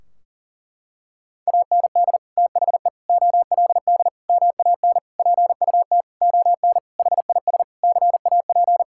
Morse code, heard as a keyed tone, about 40 wpm, 700 Hz. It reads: AND THE OLD MAN PUT ON HIS CAP
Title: Sentences from Top 1600 Words 40wpm